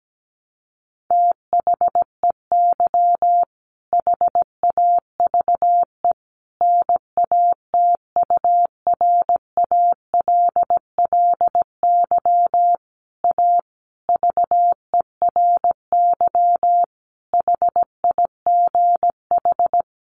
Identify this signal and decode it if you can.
Morse code, heard as a keyed tone, about 17 wpm, 700 Hz. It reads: THEY HAVE NATURALLY A VERY HIGH
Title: Sentences from Top 1300 Words 17wpm